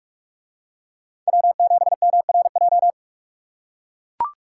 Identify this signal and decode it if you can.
Morse code, heard as a keyed tone, about 44 wpm, 700 Hz. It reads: W7MRJ